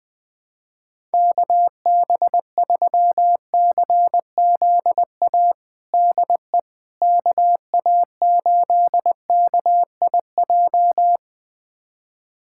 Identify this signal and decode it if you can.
Morse code, heard as a keyed tone, about 20 wpm, 700 Hz. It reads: KB3CZA DE KA8KIJ
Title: USA Callsign DE Callsign - 6 Characters - 20wpm